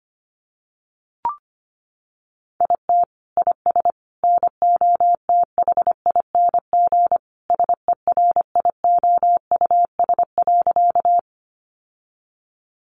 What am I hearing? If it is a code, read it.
Morse code, heard as a keyed tone, about 25 wpm, 700 Hz. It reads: IT IS NOTHING SERIOUS.
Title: Sentences from Top 1100 Words 25wpm